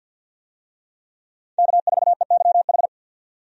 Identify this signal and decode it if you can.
Morse code, heard as a keyed tone, about 52 wpm, 700 Hz. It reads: K4EYH
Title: USA Call Signs - 5 Characters - 50wpm